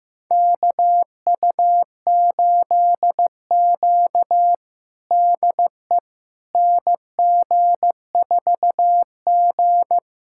Morse code, heard as a keyed tone, about 15 wpm, 700 Hz. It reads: KU8Q DE NG4G
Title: USA Callsign DE Callsign - 4 Characters - 15wpm